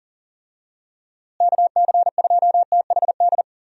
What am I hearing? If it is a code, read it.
Morse code, heard as a keyed tone, about 40 wpm, 700 Hz. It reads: KC2THD
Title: USA Call Signs - 6 Characters - Repeated 3x - 40wpm